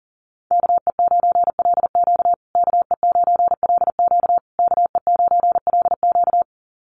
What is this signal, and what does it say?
Morse code, heard as a keyed tone, about 40 wpm, 700 Hz. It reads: KE9LQ KE9LQ KE9LQ
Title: USA Call Signs Repeated 3x 40wpm